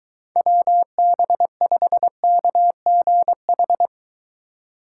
Morse code, heard as a keyed tone, about 23 wpm, 700 Hz. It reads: WB5KGH